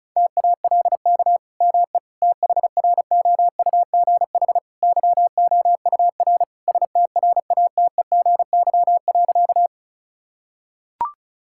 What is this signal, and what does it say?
Morse code, heard as a keyed tone, about 36 wpm, 700 Hz. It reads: TALK ME THROUGH YOUR STRATEGY.
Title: Sentences from Top 1000 Words - Review Part 5 - 35wpm